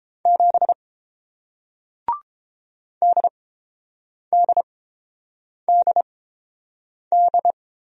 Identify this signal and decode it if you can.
Morse code, heard as a keyed tone, about 30 wpm, spaced as for 7 wpm, 700 Hz. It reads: 7 DDDD